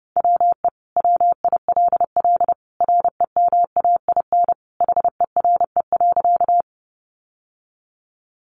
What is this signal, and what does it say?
Morse code, heard as a keyed tone, about 30 wpm, 700 Hz. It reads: WE WILL REMAIN HERE.
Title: Sentences from Top 1200 Words 30wpm